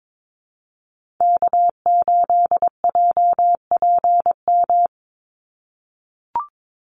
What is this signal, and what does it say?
Morse code, heard as a keyed tone, about 22 wpm, 700 Hz. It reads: K8JPM
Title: Call Signs - 22wpm